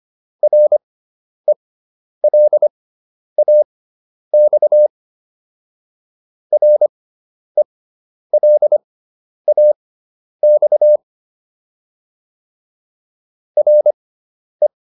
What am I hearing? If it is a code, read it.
Morse code, heard as a keyed tone, about 25 wpm, spaced as for 10 wpm, 600 Hz. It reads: RELAX RELAX RE